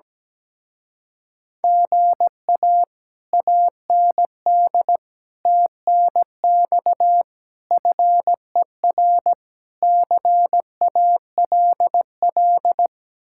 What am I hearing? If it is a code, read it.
Morse code, heard as a keyed tone, about 17 wpm, 700 Hz. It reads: GA AND TNX FER CALL
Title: Speed-Racing Common QSO Phrases - 17wpm